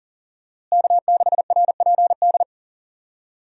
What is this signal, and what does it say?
Morse code, heard as a keyed tone, about 40 wpm, 700 Hz. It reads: K6RPD
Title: USA Call Signs - 5 Characters - Repeated 2x - 40wpm